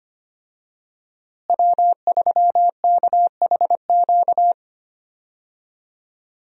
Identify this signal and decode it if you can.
Morse code, heard as a keyed tone, about 25 wpm, 700 Hz. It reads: W3KHQ
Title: USA Call Signs - 5 Characters - Repeated 2x - 25wpm